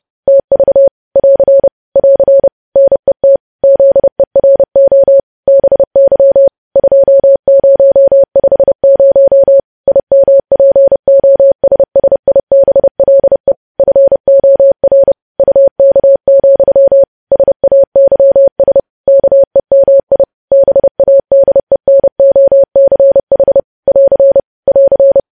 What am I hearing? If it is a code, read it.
Morse code, heard as a keyed tone, about 30 wpm, 550 Hz. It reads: TV <AR> <AR> NET ZERO BY 2050 IMPOSSIBLE FOR UK, SAYS KEMI BADENOCH <AR> <AR>
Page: news headlines in morse code at 30 wpm